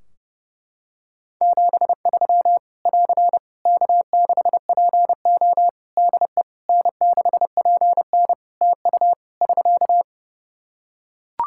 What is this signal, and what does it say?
Morse code, heard as a keyed tone, about 30 wpm, 700 Hz. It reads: 73 <AR> K6PO DE N6PN TU <SK>
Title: K1USN SST QSO Practice 30wpm